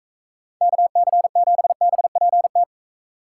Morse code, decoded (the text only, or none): KC7BPT